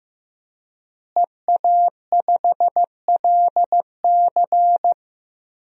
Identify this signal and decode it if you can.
Morse code, heard as a keyed tone, about 15 wpm, 700 Hz. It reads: EA5LC
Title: DXCC-100 Call Signs - 5 Characters - Repeated 2x - 15wpm